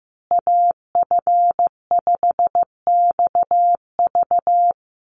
Morse code, heard as a keyed tone, about 15 wpm, 700 Hz. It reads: AF5XV
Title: USA Call Signs Repeated 2x 15wpm